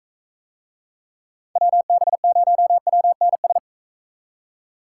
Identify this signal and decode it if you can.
Morse code, heard as a keyed tone, about 45 wpm, 700 Hz. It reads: WB0WNS